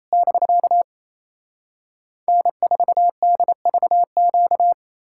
Morse code, heard as a keyed tone, about 37 wpm, 700 Hz. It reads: <BK> N4DVQ